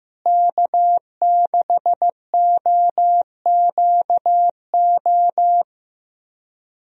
Morse code, heard as a keyed tone, about 15 wpm, 700 Hz. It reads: K6OQO
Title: USA Call Signs - 5 Characters - Repeated 3x - 15wpm